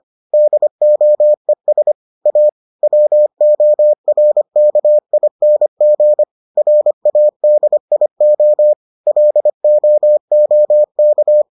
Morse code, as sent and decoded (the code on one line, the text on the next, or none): -.. --- . ... / .- / .-- --- .-. -.- .. -. --. / .-. .- -.. .. --- / .-.. --- --- -.-
DOES A WORKING RADIO LOOK